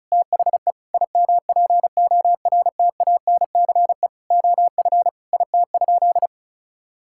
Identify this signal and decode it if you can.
Morse code, heard as a keyed tone, about 36 wpm, 700 Hz. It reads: THE IMPORTANCE OF IT?
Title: Sentences from Top 1600 Words 35wpm